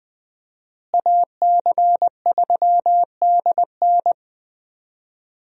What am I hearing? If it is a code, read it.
Morse code, heard as a keyed tone, about 20 wpm, 700 Hz. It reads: AC3DN
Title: USA Call Signs - 5 Characters - Repeated 3x - 20wpm